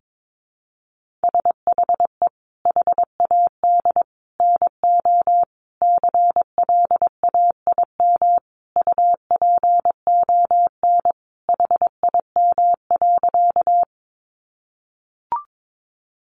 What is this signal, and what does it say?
Morse code, heard as a keyed tone, about 22 wpm, 700 Hz. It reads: SHE HAD NO CLAIM UPON HIM.
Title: Sentences from Top 1500 Words 22wpm